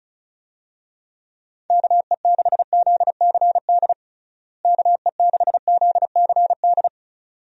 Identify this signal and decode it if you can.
Morse code, heard as a keyed tone, about 36 wpm, 700 Hz. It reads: KE6ZCD KE6ZCD
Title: USA Call Signs - 6 Characters - Repeated 2x - 35wpm